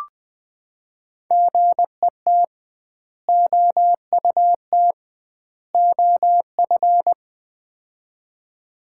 Morse code, heard as a keyed tone, about 21 wpm, 700 Hz. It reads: GET OUT OF
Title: 2x Word-Spacing - Sets of 3 Words - 20wpm